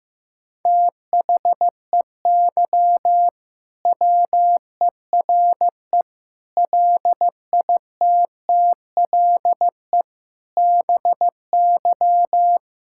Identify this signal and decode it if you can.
Morse code, heard as a keyed tone, about 15 wpm, 700 Hz. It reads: THEY WERE LITTLE BY
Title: Sentences from Top 1600 Words 15wpm